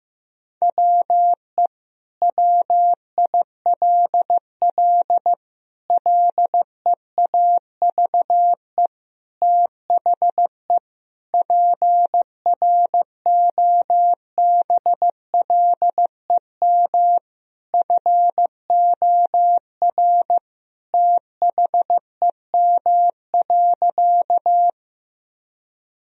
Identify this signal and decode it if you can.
Morse code, heard as a keyed tone, about 15 wpm, 700 Hz. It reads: WE WILL LEAVE THE PROBLEM FOR THEM.
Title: Sentences from Top 1000 Words - Review Part 2 - 15wpm